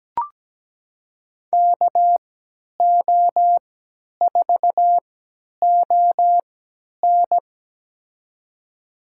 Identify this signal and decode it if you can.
Morse code, heard as a keyed tone, about 17 wpm, spaced as for 10 wpm, 700 Hz. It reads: KO4ON
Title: USA Call Signs - 3x Character Spacing - 17wpm